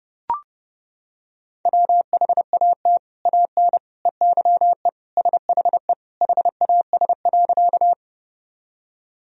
Morse code, heard as a keyed tone, about 32 wpm, 700 Hz. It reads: WHAT AN EYE SHE HAS.
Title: Sentences from Top 1000 Words - Review Part 1 - 30wpm